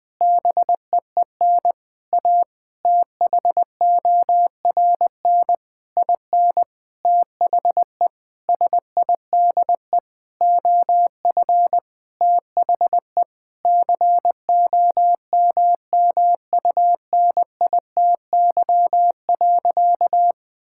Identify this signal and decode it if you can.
Morse code, heard as a keyed tone, about 20 wpm, 700 Hz. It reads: BEEN A THORN IN THE SIDE OF THE COMMUNITY.
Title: Sentences with 8-Word Phrases - Spoken Beginning - 20wpm